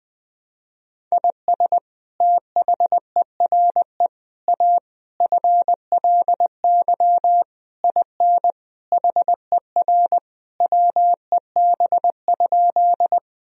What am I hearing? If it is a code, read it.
Morse code, heard as a keyed tone, about 20 wpm, 700 Hz. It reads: IS THERE A FLY IN HER WEB?
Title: Sentences from Top 1000 Words - Review Part 4 - 20wpm